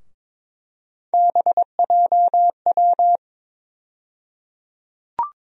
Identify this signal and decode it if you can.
Morse code, heard as a keyed tone, about 22 wpm, 700 Hz. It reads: BJW
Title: USA Call Sign Postfixes - 3 Letters 22wpm